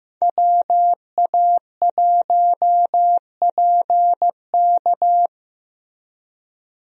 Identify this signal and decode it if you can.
Morse code, heard as a keyed tone, about 15 wpm, 700 Hz. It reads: WA1PK
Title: USA Call Signs - 5 Characters - Repeated 2x - 15wpm